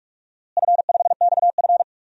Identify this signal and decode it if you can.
Morse code, heard as a keyed tone, about 46 wpm, 700 Hz. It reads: F5XF
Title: DXCC-100 Call Signs - 4 Characters - Repeated 3x - 45wpm